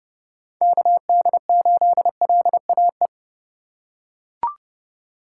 Morse code, heard as a keyed tone, about 28 wpm, 700 Hz. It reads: KD8LAE